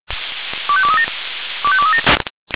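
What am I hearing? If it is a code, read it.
Five-tone selective call (ZVEI), heard as quick runs of five tones, tones of about 70 ms. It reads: 25237 25237
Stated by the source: ZVEI five-tone selective call received on SDR.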